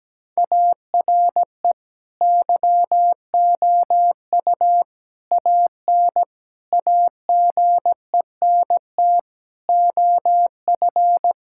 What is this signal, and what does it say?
Morse code, heard as a keyed tone, about 17 wpm, 700 Hz. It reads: ARE YOU AN AGENT OF
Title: Sentences from Top 1600 Words 17wpm